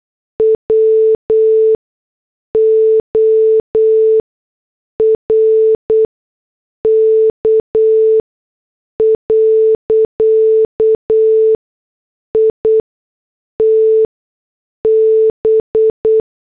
Morse code, heard as a keyed tone, about 8 wpm, spaced as for 6 wpm, 440 Hz. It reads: WORK. IT B
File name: morse / headlines